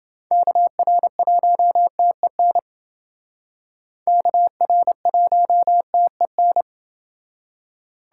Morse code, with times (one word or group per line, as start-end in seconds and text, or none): 0.3-2.6 KR1TEN
4.1-6.6 KR1TEN